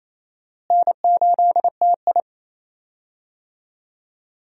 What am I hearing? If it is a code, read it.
Morse code, heard as a keyed tone, about 27 wpm, 700 Hz. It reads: N8TI
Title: USA Call Signs - 4 Characters - Repeated 2x - 28wpm